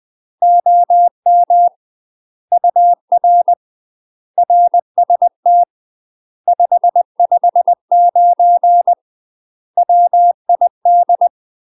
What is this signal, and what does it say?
Morse code, heard as a keyed tone, about 20 wpm, 700 Hz. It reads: OM UR RST 559 WID
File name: UK MASTER QSOs TRAINING/20WPM